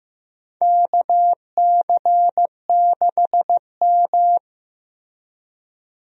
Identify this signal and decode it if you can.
Morse code, heard as a keyed tone, about 15 wpm, 700 Hz. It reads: KC6M
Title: USA Call Signs - 4 Characters - Repeated 3x - 15wpm